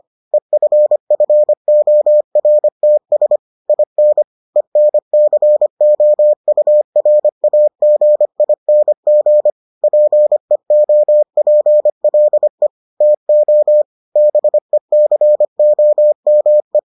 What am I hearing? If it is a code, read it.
Morse code, heard as a keyed tone, about 25 wpm, 600 Hz. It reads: EFFORTS IN ENCOURAGING PEOPLE TO BECOME